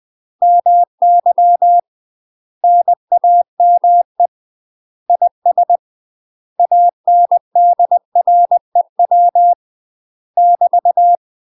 Morse code, heard as a keyed tone, about 20 wpm, 700 Hz. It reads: MY NAME IS ANDREW <BT>
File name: UK MASTER QSOs TRAINING / 20WPM / UK MASTER QSOs 0051